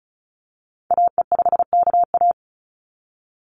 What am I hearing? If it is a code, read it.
Morse code, heard as a keyed tone, about 34 wpm, 700 Hz. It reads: AE5KA
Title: NA SOTA Activator Callsigns 35wpm